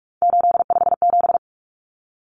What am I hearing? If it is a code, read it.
Morse code, heard as a keyed tone, about 44 wpm, 700 Hz. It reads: 857
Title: Common Cut Numbers - 3-Digit 45wpm